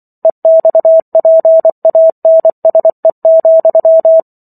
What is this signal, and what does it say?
Morse code, heard as a keyed tone, about 24 wpm, 650 Hz. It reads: EXPANSE,